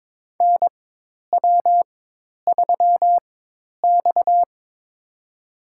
Morse code, heard as a keyed tone, about 22 wpm, spaced as for 10 wpm, 700 Hz. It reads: NW3X
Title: USA Call Signs - 4x Character Spacing - 22wpm